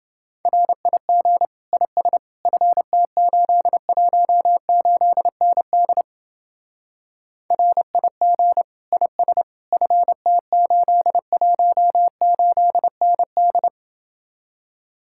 Morse code, heard as a keyed tone, about 28 wpm, 700 Hz. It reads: RIG IS FT818ND RIG IS FT818ND